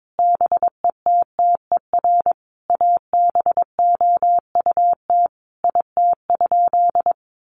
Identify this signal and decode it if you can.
Morse code, heard as a keyed tone, about 22 wpm, 700 Hz. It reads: BETTER ABOUT IT?
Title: Sentences from Top 1500 Words 22wpm